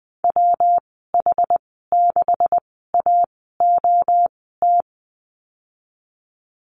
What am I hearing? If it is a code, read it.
Morse code, heard as a keyed tone, about 19 wpm, spaced as for 14 wpm, 700 Hz. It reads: WH6AOT
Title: 200 Lesson 32g - Letter k - Call Signs 20wpm